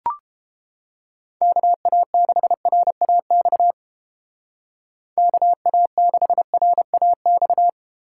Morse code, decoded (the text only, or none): KA6RAX KA6RAX